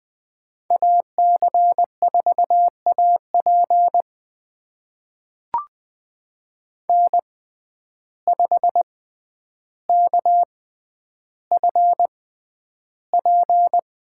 Morse code, heard as a keyed tone, about 20 wpm, spaced as for 10 wpm, 700 Hz. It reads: AC4AP N5KFP
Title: USA Call Signs - 5 Characters - 6x Character Spacing - 20wpm